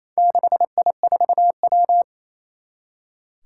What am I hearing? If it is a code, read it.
Morse code, heard as a keyed tone, about 27 wpm, 700 Hz. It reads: 6I4W
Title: Four Challenging Letter-Number Combinations 28wpm